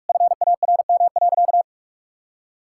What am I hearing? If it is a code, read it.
Morse code, heard as a keyed tone, about 46 wpm, 700 Hz. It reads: FARM.
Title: Sentences from Top 1000 Words - Review Part 6 - 45wpm